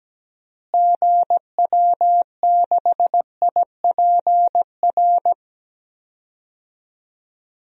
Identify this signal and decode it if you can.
Morse code, heard as a keyed tone, about 17 wpm, 700 Hz. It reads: GW6IPR
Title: DXCC-100 Call Signs - 6 Characters - Repeated 3x - 17wpm